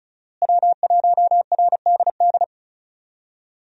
Morse code, heard as a keyed tone, about 34 wpm, 700 Hz. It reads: W1RDD
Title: USA Call Signs - 5 Characters - Repeated 3x - 35wpm